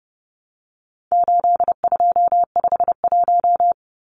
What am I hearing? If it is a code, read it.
Morse code, heard as a keyed tone, about 30 wpm, 700 Hz. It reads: 8251
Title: Numbers - 4-Digit 30wpm